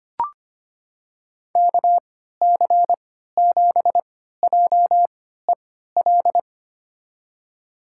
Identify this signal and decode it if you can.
Morse code, heard as a keyed tone, about 25 wpm, spaced as for 14 wpm, 700 Hz. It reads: KC7JEL
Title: USA Call Signs - 3x Character Spacing - 25wpm